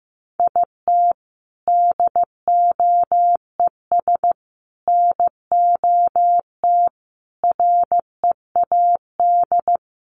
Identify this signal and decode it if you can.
Morse code, heard as a keyed tone, about 14 wpm, 700 Hz. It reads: IT DOES NOT READ
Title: Sentences from Top 800 Words 15wpm